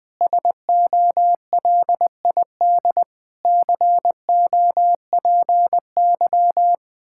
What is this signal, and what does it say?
Morse code, heard as a keyed tone, about 20 wpm, 700 Hz. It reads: SOLID COPY
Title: Practice QSO - Exchange 3 - More Info - 20wpm